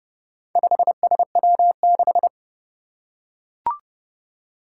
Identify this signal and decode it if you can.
Morse code, heard as a keyed tone, about 28 wpm, 700 Hz. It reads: 5SW6
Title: Four Challenging Letter-Number Combinations 30wpm